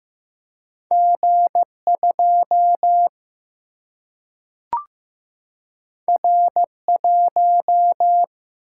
Morse code, heard as a keyed tone, about 15 wpm, 700 Hz. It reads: G2 R1
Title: Single-Letter Single-Number 15wpm